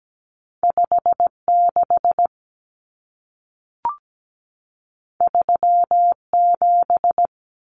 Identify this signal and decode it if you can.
Morse code, heard as a keyed tone, about 17 wpm, 700 Hz. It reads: 56 37
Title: Numbers - 2-Digit 17wpm